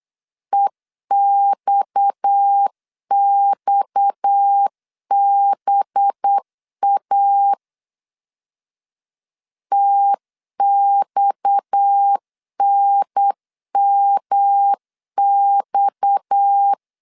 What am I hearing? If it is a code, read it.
Morse code, heard as a keyed tone, about 8 wpm, 800 Hz. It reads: EXXBA TXNMX
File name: LEZIONI / 6 PKX0 / 6 PKX 8wpm 3041